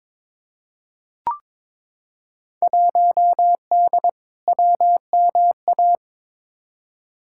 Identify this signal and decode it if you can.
Morse code, heard as a keyed tone, about 22 wpm, 700 Hz. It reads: 1D WMA
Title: Field Day Exchanges - 22wpm